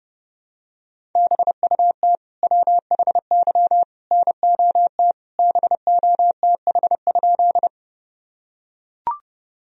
Morse code, heard as a keyed tone, about 30 wpm, 700 Hz. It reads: BUT WHY NOT BOTH?
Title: Sentences from Top 1000 Words - Review Part 1 - 30wpm